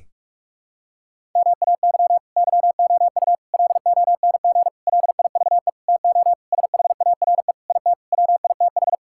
Morse code, heard as a keyed tone, about 46 wpm, 700 Hz. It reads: MAY YOU LONG LIVE TO SHARE IT WITH
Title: Sentences from Top 1000 Words - Review Part 6 - 45wpm